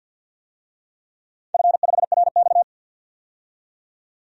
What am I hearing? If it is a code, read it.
Morse code, heard as a keyed tone, about 48 wpm, 700 Hz. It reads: F5RX
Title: DXCC-100 Call Signs - 4 Characters - 50wpm